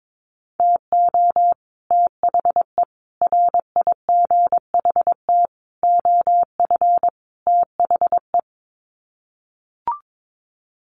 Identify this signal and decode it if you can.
Morse code, heard as a keyed tone, about 22 wpm, 700 Hz. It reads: TO THE RIGHT OF THE
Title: Sets of 5 Words Encore 22wpm